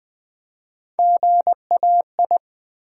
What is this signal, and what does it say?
Morse code, heard as a keyed tone, about 20 wpm, 700 Hz. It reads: GAI
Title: Build A Word - 1 Letter At A Time - 4-Letter Words - 20wpm